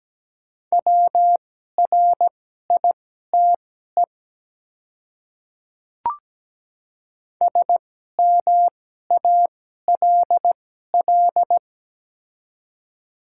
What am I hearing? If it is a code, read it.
Morse code, heard as a keyed tone, about 17 wpm, spaced as for 12 wpm, 700 Hz. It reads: WRITE SMALL
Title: 2x Character Spacing - Top 200 Words - 17wpm